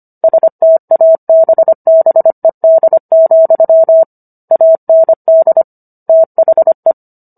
Morse code, heard as a keyed tone, about 25 wpm, 650 Hz. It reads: STABBED, AND THE